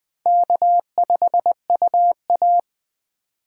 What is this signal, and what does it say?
Morse code, heard as a keyed tone, about 20 wpm, 700 Hz. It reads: K5UA
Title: Speed Racing - USA 1x2 Call Signs 17wpm